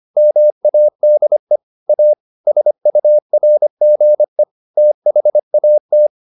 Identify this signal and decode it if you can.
Morse code, heard as a keyed tone, about 25 wpm, 600 Hz. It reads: MADE A SURGE THAT
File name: the old man and the sea part 1 25wpm